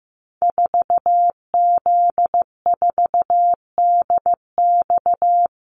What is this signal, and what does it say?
Morse code, heard as a keyed tone, about 15 wpm, 700 Hz. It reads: 4Z4DX
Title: Call Signs Repeated 2x - 15wpm